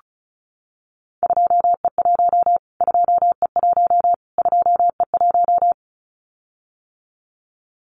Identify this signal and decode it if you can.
Morse code, heard as a keyed tone, about 34 wpm, 700 Hz. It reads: 2E1 2E1 2E1
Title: Callsign Prefixes for Top 25 DXCC - Repeated 3x - 35wpm